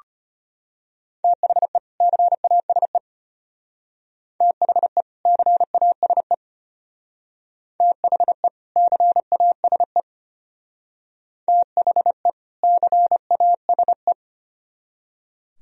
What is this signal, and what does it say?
Morse code, heard as a keyed tone, about 31 wpm, 700 Hz. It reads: THE CASE THE CASE THE CASE THE CASE
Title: Speed-Racing - Sets of 2 Words 25wpm